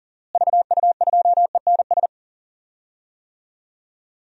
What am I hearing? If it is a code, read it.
Morse code, heard as a keyed tone, about 40 wpm, 700 Hz. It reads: VU2ENS
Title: DXCC-100 Call Signs - 6 Characters - Repeated 3x - 40wpm